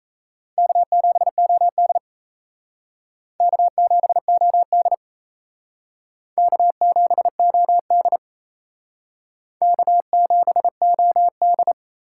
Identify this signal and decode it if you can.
Morse code, heard as a keyed tone, about 36 wpm, 700 Hz. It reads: K7OD K7OD K7OD K7OD